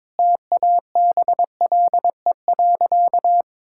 Morse code, heard as a keyed tone, about 22 wpm, 700 Hz. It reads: TABLE.
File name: practice / 22wpm / Sentences with 3-Word Phrases - 22wpm